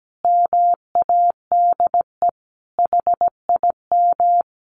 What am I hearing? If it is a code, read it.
Morse code, heard as a keyed tone, about 17 wpm, 700 Hz. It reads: MADE HIM
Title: Sentences from Top 1400 Words 17wpm